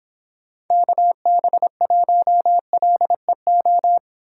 Morse code, heard as a keyed tone, about 31 wpm, 700 Hz. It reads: KB1LEO